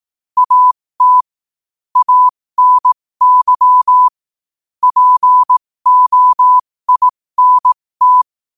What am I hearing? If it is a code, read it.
Morse code, heard as a keyed tone, about 18 wpm, spaced as for 15 wpm, 1000 Hz. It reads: AT ANY POINT
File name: qotdfiles / 15wpm